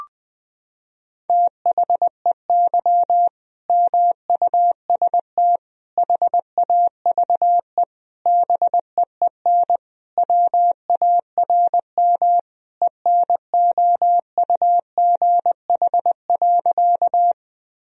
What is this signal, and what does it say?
Morse code, heard as a keyed tone, about 21 wpm, 700 Hz. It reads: THEY MUST HAVE BEEN WARM ENOUGH.
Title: Sentences from Top 1000 Words - Review Part 2 - 20wpm